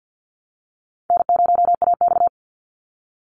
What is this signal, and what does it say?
Morse code, heard as a keyed tone, about 50 wpm, 700 Hz. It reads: N0AX
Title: Easy Call Signs - 50wpm